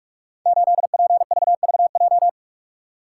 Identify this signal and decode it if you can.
Morse code, heard as a keyed tone, about 44 wpm, 700 Hz. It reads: 8PVVJ
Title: Code Groups - 5 Letter:Numbers - 45wpm